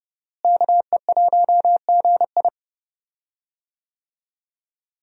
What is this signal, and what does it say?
Morse code, heard as a keyed tone, about 30 wpm, 700 Hz. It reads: KE1GI
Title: USA Call Signs - 5 Characters - Repeated 2x - 30wpm